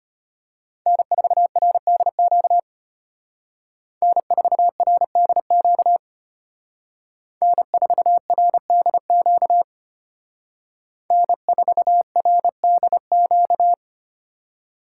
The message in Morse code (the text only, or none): N4RDQ N4RDQ N4RDQ N4RDQ